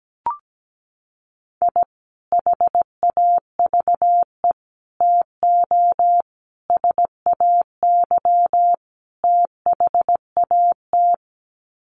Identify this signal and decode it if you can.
Morse code, heard as a keyed tone, about 17 wpm, 700 Hz. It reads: I HAVE TO SAY THAT
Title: Sets of 5 Words Encore 17wpm